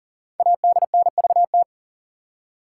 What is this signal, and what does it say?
Morse code, heard as a keyed tone, about 40 wpm, 700 Hz. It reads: ADNVT
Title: Code Groups - 5 Letters - 40wpm